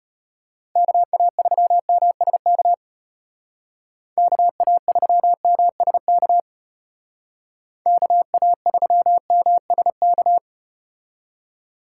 Morse code, heard as a keyed tone, about 34 wpm, 700 Hz. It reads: KA3MSK KA3MSK KA3MSK